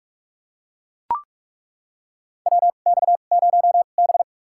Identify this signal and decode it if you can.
Morse code, heard as a keyed tone, about 43 wpm, spaced as for 31 wpm, 700 Hz. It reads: WX0B
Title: USA Call Signs - 4 Characters - 2x Character Spacing - 45wpm